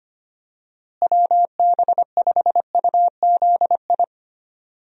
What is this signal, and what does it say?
Morse code, heard as a keyed tone, about 25 wpm, 700 Hz. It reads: WB5UZI
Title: USA Call Signs - 6 Characters - Repeated 3x - 25wpm